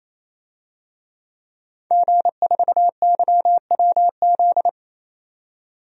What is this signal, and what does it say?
Morse code, heard as a keyed tone, about 28 wpm, 700 Hz. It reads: G4YWZ
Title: DXCC-100 Call Signs - 5 Characters - 28wpm